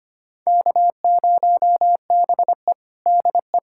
Morse code, heard as a keyed tone, about 25 wpm, 700 Hz. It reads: K0BE DE
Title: USA Callsign DE Callsign - 4 Characters - 25wpm